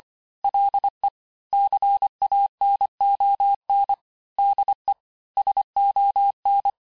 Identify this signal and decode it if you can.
Morse code, heard as a keyed tone, about 25 wpm, 800 Hz. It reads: LE CANON DE SON
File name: cours-868-teste-clair-vitesse-23mpm